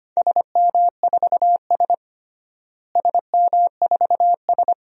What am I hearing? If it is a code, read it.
Morse code, heard as a keyed tone, about 25 wpm, 700 Hz. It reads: SM4S SM4S
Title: DXCC-100 Call Signs - 4 Characters - Repeated 3x - 25wpm